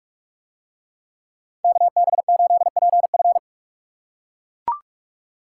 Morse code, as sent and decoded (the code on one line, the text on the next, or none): -.- -... ---.. .--. ..-.
KB8PF